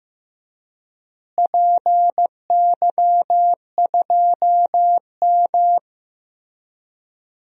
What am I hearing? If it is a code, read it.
Morse code, heard as a keyed tone, about 15 wpm, 700 Hz. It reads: PY2M
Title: DXCC-100 Call Signs - 4 Characters - Repeated 3x - 15wpm